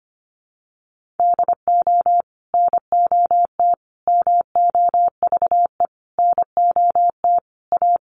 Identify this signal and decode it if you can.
Morse code, heard as a keyed tone, about 25 wpm, 700 Hz. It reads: DO NOT MOVE NOT A
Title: Sentences from Top 1300 Words 25wpm